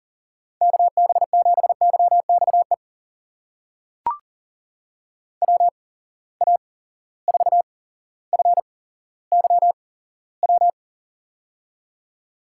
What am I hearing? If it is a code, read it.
Morse code, heard as a keyed tone, about 39 wpm, spaced as for 17 wpm, 700 Hz. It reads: KB7YXE WA4FYW
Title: USA Call Signs - 6 Characters - 8x Character Spacing - 40wpm